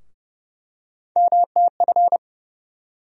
Morse code, heard as a keyed tone, about 30 wpm, 700 Hz. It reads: MTF
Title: USA Call Sign Postfixes - 3 Letters 30wpm